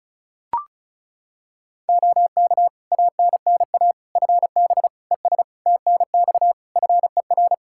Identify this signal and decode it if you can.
Morse code, heard as a keyed tone, about 34 wpm, 700 Hz. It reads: OK ANNA FB ES TNX FER
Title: Practice QSO - Exchange 2 - Info - 35wpm